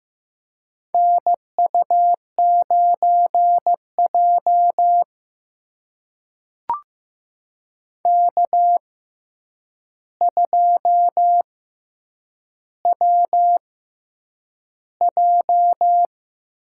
Morse code, heard as a keyed tone, about 15 wpm, spaced as for 8 wpm, 700 Hz. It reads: NU9J K2WJ
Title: USA Call Signs - 4 Characters - 6x Character Spacing - 15wpm